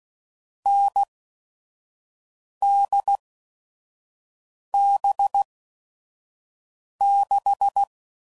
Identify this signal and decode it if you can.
Morse code, heard as a keyed tone, about 16 wpm, spaced as for 5 wpm, 800 Hz. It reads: NDB6
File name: Mixed / Short Runs / NDB NDB6